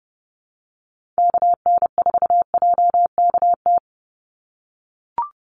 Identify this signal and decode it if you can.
Morse code, heard as a keyed tone, about 29 wpm, 700 Hz. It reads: KN4JKT
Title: USA Call Signs 30wpm